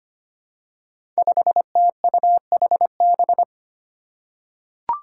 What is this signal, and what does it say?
Morse code, heard as a keyed tone, about 25 wpm, 700 Hz. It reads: 5TUHB
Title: Five Challenging Letter-Number Combinations 25wpm